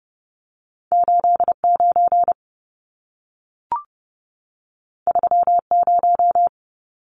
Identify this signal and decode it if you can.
Morse code, heard as a keyed tone, about 30 wpm, 700 Hz. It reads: 89 30
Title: Numbers - 2-Digit 30wpm